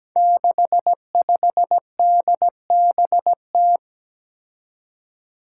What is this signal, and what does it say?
Morse code, heard as a keyed tone, about 17 wpm, 700 Hz. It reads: 65DBT
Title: Five Challenging Letter-Number Combinations 17wpm